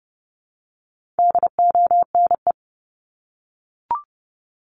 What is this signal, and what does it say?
Morse code, heard as a keyed tone, about 29 wpm, 700 Hz. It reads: DONE
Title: Top 500 Words 30wpm